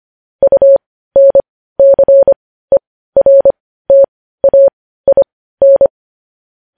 Morse code, heard as a keyed tone, about 25 wpm, spaced as for 15 wpm, 550 Hz. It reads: UNCERTAIN
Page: news-headlines-in-morse-code-at-15-wpm